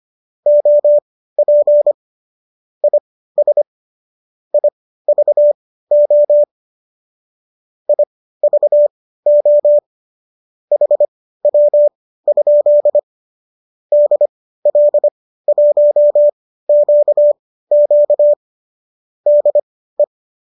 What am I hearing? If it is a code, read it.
Morse code, heard as a keyed tone, about 25 wpm, spaced as for 15 wpm, 590 Hz. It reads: OP IS IVO IVO HW? DL1QQ DE